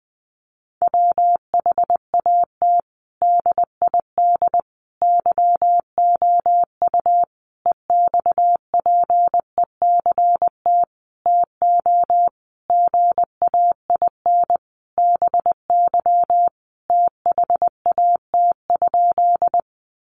Morse code, heard as a keyed tone, about 20 wpm, 700 Hz. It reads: WHAT DID YOU EXPECT TO GAIN BY THAT?
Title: Sentences from Top 1100 Words 20wpm